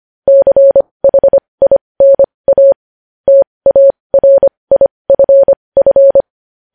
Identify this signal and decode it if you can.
Morse code, heard as a keyed tone, about 25 wpm, spaced as for 20 wpm, 550 Hz. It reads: CHINA TARIFF
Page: news headlines in morse code at 20 wpm